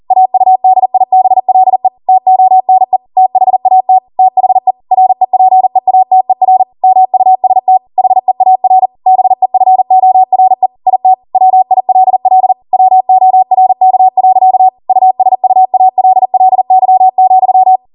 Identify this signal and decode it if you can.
Morse code, heard as a keyed tone, about 40 wpm, 750 Hz. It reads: AUDIBLE TONE THAT THE REPEATER MUST HEAR BEFORE IT WILL WORK. USUALLY,
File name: Morse / 40WPM